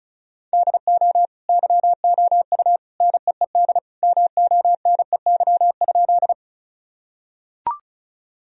Morse code, heard as a keyed tone, about 34 wpm, 700 Hz. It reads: DO YOU NEED MONEY?
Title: Sentences from Top 1000 Words - Review Part 1 - 35wpm